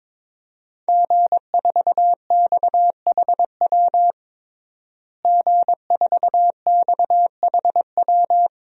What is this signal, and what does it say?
Morse code, heard as a keyed tone, about 21 wpm, 700 Hz. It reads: G4XHW G4XHW
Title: DXCC-100 Call Signs - 5 Characters - Repeated 2x - 22wpm